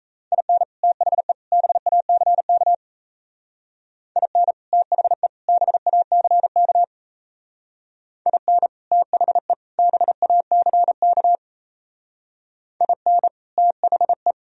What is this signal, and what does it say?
Morse code, heard as a keyed tone, about 34 wpm, 700 Hz. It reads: IN THE BACK IN THE BACK IN THE BACK IN THE
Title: Speed-Racing - Sets of 3 Words 28wpm